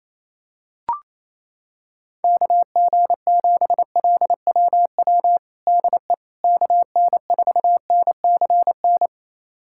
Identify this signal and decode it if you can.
Morse code, heard as a keyed tone, about 28 wpm, 700 Hz. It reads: KG7LWW DE KN4NCN
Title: USA Callsign DE Callsign 28wpm